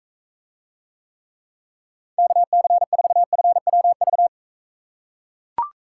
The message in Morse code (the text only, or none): KC4FWV